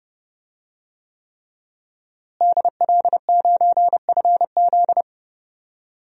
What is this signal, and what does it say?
Morse code, heard as a keyed tone, about 30 wpm, 700 Hz. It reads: DL9FZ
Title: DXCC-100 Call Signs - 5 Characters - Repeated 3x - 30wpm